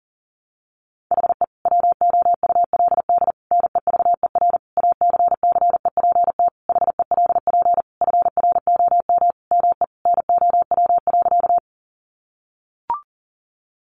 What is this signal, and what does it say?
Morse code, heard as a keyed tone, about 40 wpm, 700 Hz. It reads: HE WOULD NEVER ACCEPT HELP FROM ME NOW.
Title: Sentences from Top 1300 Words 40wpm